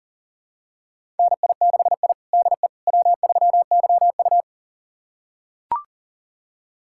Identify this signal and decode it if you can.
Morse code, heard as a keyed tone, about 40 wpm, 700 Hz. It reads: NI6I DE W3YU
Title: USA Callsign DE Callsign - 4 Characters - 40wpm